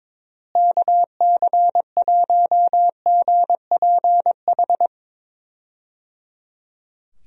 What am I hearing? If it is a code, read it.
Morse code, heard as a keyed tone, about 22 wpm, 700 Hz. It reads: KC1GPH
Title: USA Call Signs - 6 Characters - Repeated 3x - 22wpm